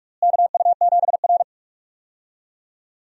Morse code, heard as a keyed tone, about 46 wpm, 700 Hz. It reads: KU7R
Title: USA Call Signs - 4 Characters - Repeated 2x - 45wpm